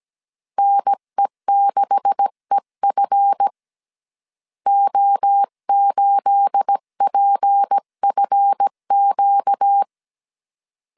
Morse code, heard as a keyed tone, about 17 wpm, 800 Hz. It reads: NE6EF O8PFQ
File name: CW / DET / 1 9 1500C S B 16wpm 2856